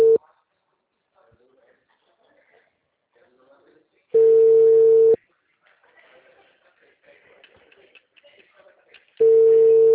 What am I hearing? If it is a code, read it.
Ringback tone (China, 450 Hz), with speech.